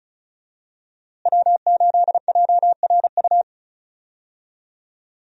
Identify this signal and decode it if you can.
Morse code, heard as a keyed tone, about 36 wpm, 700 Hz. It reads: W8JRU
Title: USA Call Signs - 5 Characters - Repeated 3x - 35wpm